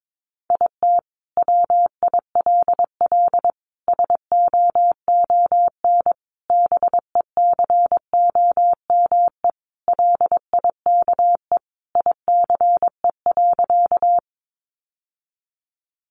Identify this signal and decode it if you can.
Morse code, heard as a keyed tone, about 22 wpm, 700 Hz. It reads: IT WILL SOON BECOME LIKE ICE.
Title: Sentences from Top 1000 Words 22wpm